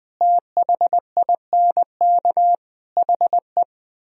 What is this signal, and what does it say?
Morse code, heard as a keyed tone, about 21 wpm, 700 Hz. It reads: THINK HE
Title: Speed-Racing - Sets of 4 Words 20wpm